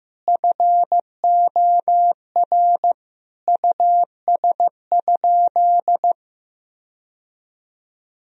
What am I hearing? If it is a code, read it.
Morse code, heard as a keyed tone, about 15 wpm, 700 Hz. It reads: FOR US?
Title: Sentences from Top 1000 Words - Review Part 4 - 15wpm